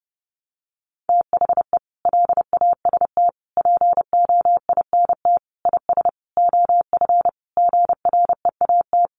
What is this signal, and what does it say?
Morse code, heard as a keyed tone, about 30 wpm, 700 Hz. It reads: THE LAST POINT IS OF GREAT
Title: Sentences from Top 1200 Words 30wpm